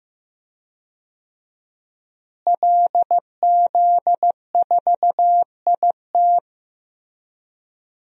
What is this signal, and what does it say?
Morse code, heard as a keyed tone, about 15 wpm, 700 Hz. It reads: LZ4IT
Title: DXCC-100 Call Signs - 5 Characters - Repeated 2x - 15wpm